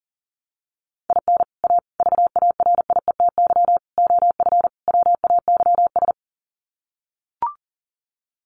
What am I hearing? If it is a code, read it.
Morse code, heard as a keyed tone, about 40 wpm, 700 Hz. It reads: IN A VARIETY OF WAYS
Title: Sets of 5 Words Encore 40wpm